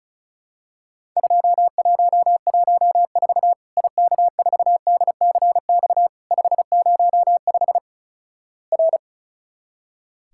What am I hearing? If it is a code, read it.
Morse code, heard as a keyed tone, about 36 wpm, 700 Hz. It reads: 2114 IK4DCX 505 R
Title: Worked All Europe DX Contest - QTC Exchange - 35wpm